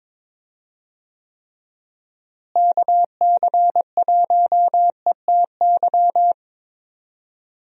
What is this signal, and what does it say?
Morse code, heard as a keyed tone, about 22 wpm, 700 Hz. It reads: KC1ETY
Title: USA Call Signs - 6 Characters - Repeated 3x - 22wpm